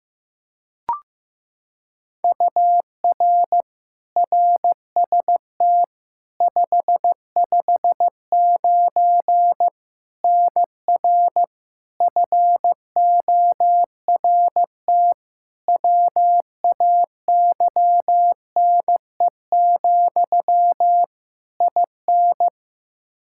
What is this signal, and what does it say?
Morse code, heard as a keyed tone, about 16 wpm, 700 Hz. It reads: UR RST 559 NR FORT WAYNE, IN
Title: Practice QSO - Exchange 1 - Beginning - 15wpm